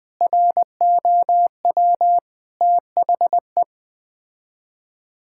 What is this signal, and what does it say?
Morse code, heard as a keyed tone, about 20 wpm, 700 Hz. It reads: ROW THE
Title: Sets of 8 Words - 20wpm